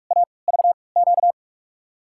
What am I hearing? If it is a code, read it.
Morse code, heard as a keyed tone, about 42 wpm, spaced as for 25 wpm, 700 Hz. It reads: AVQ
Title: USA Call Signs - 6 Characters - 3x Character Spacing - 45wpm